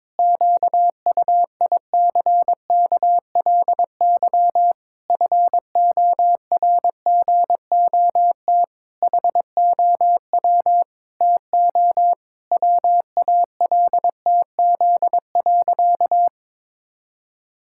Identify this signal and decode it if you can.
Morse code, heard as a keyed tone, about 22 wpm, 700 Hz. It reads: QUICKLY FORGOT HOW TO WALTZ.
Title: Pangram Challenge v2 - 22wpm